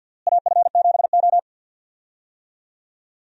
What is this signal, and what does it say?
Morse code, heard as a keyed tone, about 52 wpm, 700 Hz. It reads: AF7O
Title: USA Call Signs - 4 Characters - Repeated 2x - 50wpm